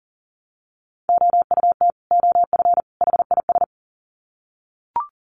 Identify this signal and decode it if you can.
Morse code, heard as a keyed tone, about 40 wpm, 700 Hz. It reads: OUT OF HIS
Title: Sets of 3 Words 40wpm